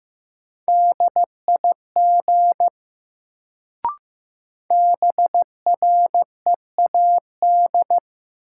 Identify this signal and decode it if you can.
Morse code, heard as a keyed tone, about 15 wpm, 700 Hz. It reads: DIG BREAD
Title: Sentences from Top 1600 Words - New Word Drill - 15wpm